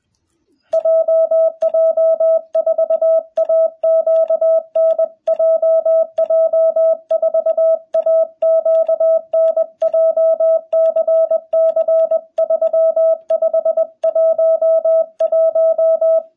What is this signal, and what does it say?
Morse code, heard as a keyed tone, about 21 wpm, 660 Hz. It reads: JJ4AQN JJ4AQN JCC 3511
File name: jh8cbh / 2021cw